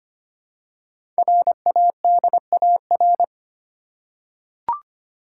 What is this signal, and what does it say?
Morse code, heard as a keyed tone, about 25 wpm, 700 Hz. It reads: RADAR
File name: 5 Letter Words - First 3 Letters Given - 25wpm